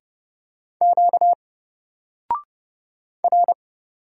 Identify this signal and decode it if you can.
Morse code, heard as a keyed tone, about 30 wpm, 700 Hz. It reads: Q R
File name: Single-Letter-Number-30wpm